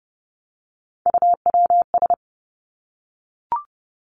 Morse code, heard as a keyed tone, about 30 wpm, 700 Hz. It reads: UWS
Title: Three Challenging Letter-Number Combinations 30wpm